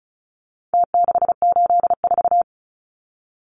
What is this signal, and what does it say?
Morse code, heard as a keyed tone, about 34 wpm, 700 Hz. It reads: T684
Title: Numbers - Contest Style - 4-Digit 35wpm